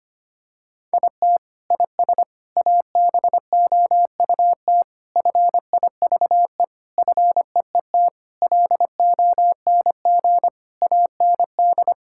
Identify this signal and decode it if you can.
Morse code, heard as a keyed tone, about 25 wpm, 700 Hz. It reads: IT IS ABOUT FIVE FEET LONG AND
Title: Sentences from Top 1000 Words - Review Part 6 - 25wpm